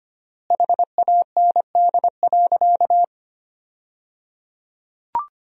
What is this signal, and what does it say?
Morse code, heard as a keyed tone, about 25 wpm, 700 Hz. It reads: HAND.
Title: 3x Word-Spacing - Sentences from Top 200 Words - 25wpm